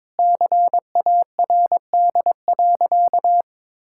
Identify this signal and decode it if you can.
Morse code, heard as a keyed tone, about 21 wpm, 700 Hz. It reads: CARD.
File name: Sentences with 7-Word Phrases - 22wpm